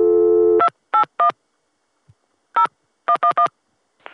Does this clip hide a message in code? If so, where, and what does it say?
Yes: DTMF from 0.6 to 3.5 s, heard as telephone touch tones, tones of about 90 ms. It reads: 281 0111